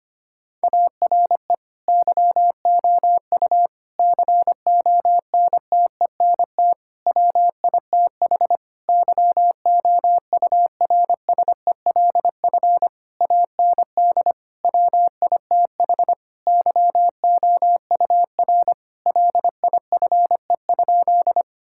Morse code, heard as a keyed tone, about 25 wpm, 700 Hz. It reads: ARE YOU CONTENT WITH YOURSELF AND WITH YOUR LIFE?
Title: Sentences from Top 1000 Words - Review Part 6 - 25wpm